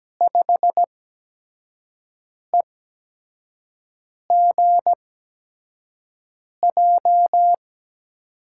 Morse code, heard as a keyed tone, about 17 wpm, spaced as for 5 wpm, 700 Hz. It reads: 5EGJ
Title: USA Call Signs - 6 Characters - 8x Character Spacing - 17wpm